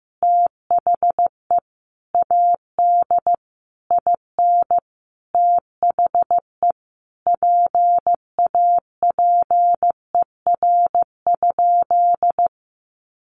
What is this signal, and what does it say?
Morse code, heard as a keyed tone, about 15 wpm, 700 Hz. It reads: THE AD IN THE PAPER?
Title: Sentences from Top 1300 Words 15wpm